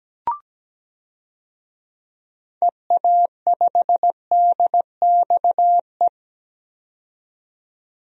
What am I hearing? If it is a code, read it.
Morse code, heard as a keyed tone, about 17 wpm, 700 Hz. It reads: EA5DXE
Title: DXCC-100 Call Signs - 6 Characters - 17wpm